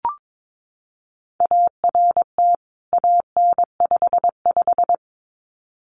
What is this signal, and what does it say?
Morse code, heard as a keyed tone, about 22 wpm, 700 Hz. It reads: ART AN55
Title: CWT Exchange 22wpm